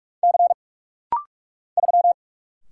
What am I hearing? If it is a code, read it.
Morse code, heard as a keyed tone, about 44 wpm, 700 Hz. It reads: C 3
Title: Single Letter-Number Alt - Rapid-Fire 45wpm